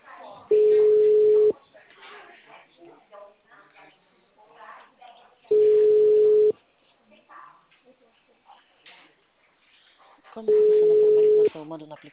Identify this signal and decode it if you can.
Ringback tone (Europe, 425 Hz), with speech.